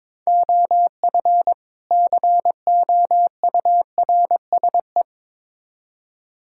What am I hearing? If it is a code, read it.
Morse code, heard as a keyed tone, about 22 wpm, 700 Hz. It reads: OF COURSE
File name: Build A Phrase - 2-Word Phrases - 22wpm